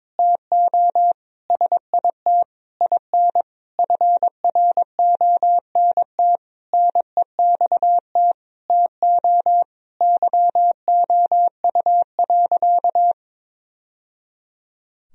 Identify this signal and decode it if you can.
Morse code, heard as a keyed tone, about 22 wpm, 700 Hz. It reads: TO SIT IN FRONT NEXT TO YOU.
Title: Sentences from Top 1000 Words - Review Part 2 - 22wpm